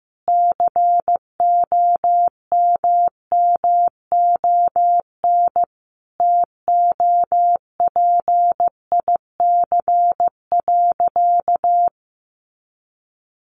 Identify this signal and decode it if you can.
Morse code, heard as a keyed tone, about 15 wpm, 700 Hz. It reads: COMMON TOPIC.
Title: Sentences from Top 1200 Words 15wpm